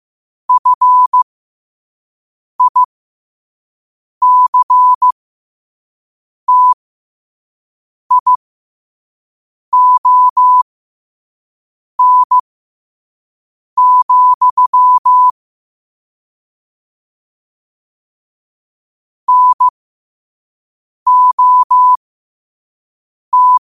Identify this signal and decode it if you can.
Morse code, heard as a keyed tone, about 15 wpm, spaced as for 5 wpm, 1000 Hz. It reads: FICTION, NOT
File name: qotdfiles / 5wpm